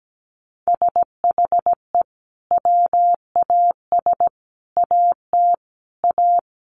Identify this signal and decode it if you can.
Morse code, heard as a keyed tone, about 17 wpm, 700 Hz. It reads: SHE WAS AT A
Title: Sentences from Top 1300 Words 17wpm